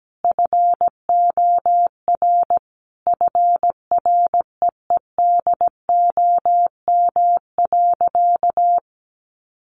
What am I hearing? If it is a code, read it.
Morse code, heard as a keyed tone, about 17 wpm, 700 Hz. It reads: FOR FREEDOM.
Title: Sentences from Top 1100 Words 17wpm